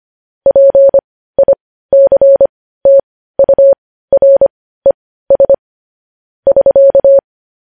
Morse code, heard as a keyed tone, about 25 wpm, spaced as for 15 wpm, 550 Hz. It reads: PICTURES <SK>